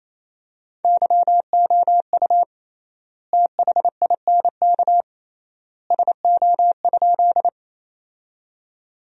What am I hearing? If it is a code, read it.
Morse code, heard as a keyed tone, about 27 wpm, 700 Hz. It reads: YOU THINK SO?
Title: 3x Word-Spacing - Sentences from Top 200 Words - 28wpm